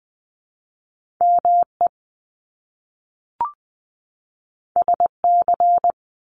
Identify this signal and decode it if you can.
Morse code, heard as a keyed tone, about 20 wpm, 700 Hz. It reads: ME SC
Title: Top Two Letter Combinations in Words 20wpm